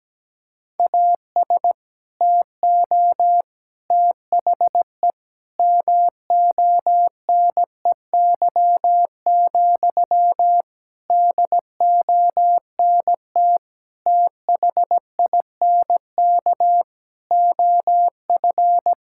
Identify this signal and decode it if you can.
Morse code, heard as a keyed tone, about 17 wpm, 700 Hz. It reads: AS TO THE MONEY, DONT THINK OF
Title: Sentences from Top 1000 Words - Review Part 5 - 17wpm